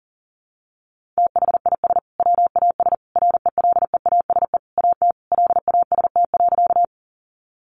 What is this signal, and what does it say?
Morse code, heard as a keyed tone, about 37 wpm, 700 Hz. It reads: THIS WAS RELEASE AT LAST.
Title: Sentences from Top 1100 Words 40wpm